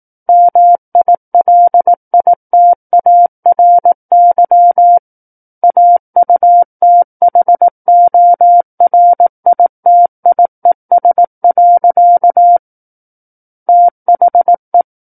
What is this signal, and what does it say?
Morse code, heard as a keyed tone, about 18 wpm, 700 Hz. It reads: MILITARY AUTHORITIES. THE
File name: 18 WPM